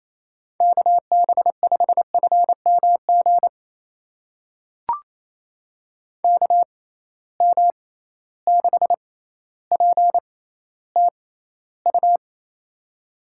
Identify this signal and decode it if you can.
Morse code, heard as a keyed tone, about 28 wpm, spaced as for 14 wpm, 700 Hz. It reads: KB5FMG KM6PTU
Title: USA Call Signs - 6 Characters - 6x Character Spacing - 28wpm